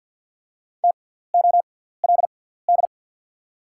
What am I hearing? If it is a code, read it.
Morse code, heard as a keyed tone, about 48 wpm, spaced as for 17 wpm, 700 Hz. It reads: TOLD
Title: 6x Character Spacing - Top 500 Words - 50wpm